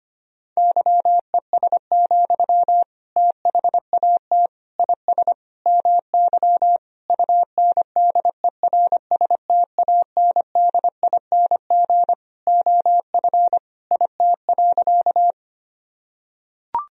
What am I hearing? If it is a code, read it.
Morse code, heard as a keyed tone, about 25 wpm, 700 Hz. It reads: YES, THAT IS MY UNDERSTANDING OF IT.
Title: Sentences from Top 1000 Words - Review Part 3 - 25wpm